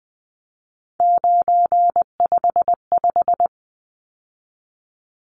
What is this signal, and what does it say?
Morse code, heard as a keyed tone, about 20 wpm, 700 Hz. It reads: 955
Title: Numbers - 3-Digit 20wpm